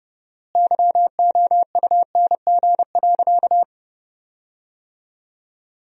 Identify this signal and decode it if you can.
Morse code, heard as a keyed tone, about 32 wpm, 700 Hz. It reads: YOUNG.
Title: 4x Word-Spacing - Sentences from Top 400 Words - 30wpm